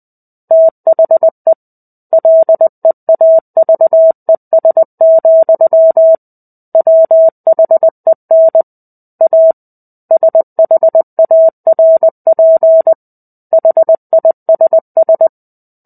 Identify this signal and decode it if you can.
Morse code, heard as a keyed tone, about 20 wpm, 650 Hz. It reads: THE LEAVES, WHEN A SHARP HISS
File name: 05-Chapter-05 20 WPM